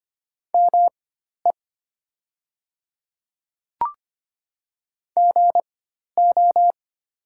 Morse code, heard as a keyed tone, about 25 wpm, spaced as for 12 wpm, 700 Hz. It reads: ME GO
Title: 4x Character Spacing - Top 300 Words - 25wpm